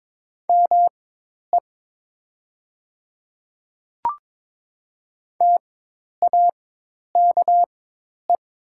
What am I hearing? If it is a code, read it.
Morse code, heard as a keyed tone, about 22 wpm, spaced as for 10 wpm, 700 Hz. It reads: ME TAKE
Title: 4x Character Spacing - Top 500 Words - 22wpm